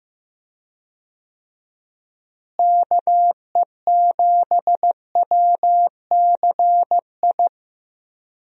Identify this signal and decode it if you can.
Morse code, heard as a keyed tone, about 15 wpm, 700 Hz. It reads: KE7WCI
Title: USA Call Signs - 6 Characters - Repeated 3x - 15wpm